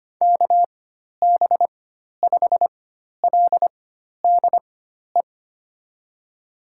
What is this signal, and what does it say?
Morse code, heard as a keyed tone, about 25 wpm, spaced as for 12 wpm, 700 Hz. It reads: KB5LDE
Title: USA Call Signs - 4x Character Spacing - 25wpm